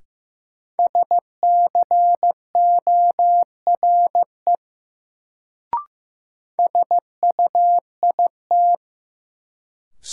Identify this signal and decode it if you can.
Morse code, heard as a keyed tone, about 15 wpm, 700 Hz. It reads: SCORE SUIT
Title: Sentences from Top 1600 Words - New Word Drill Alt - 15wpm